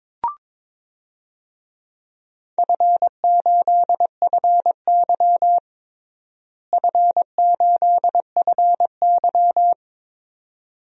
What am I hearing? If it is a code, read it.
Morse code, heard as a keyed tone, about 22 wpm, 700 Hz. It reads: F8FY F8FY
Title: DXCC-100 Call Signs - 4 Characters - Repeated 2x - 22wpm